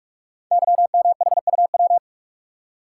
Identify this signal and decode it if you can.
Morse code, heard as a keyed tone, about 44 wpm, 700 Hz. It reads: YMHUW
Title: Code Groups - 5 Letter:Numbers - 45wpm